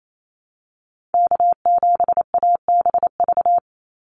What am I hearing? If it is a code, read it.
Morse code, heard as a keyed tone, about 28 wpm, 700 Hz. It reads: K7ABV
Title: Call Signs Repeated 3x - 28wpm